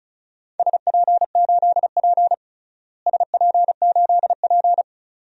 Morse code, heard as a keyed tone, about 36 wpm, 700 Hz. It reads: SP8P SP8P
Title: DXCC-100 Call Signs - 4 Characters - Repeated 2x - 35wpm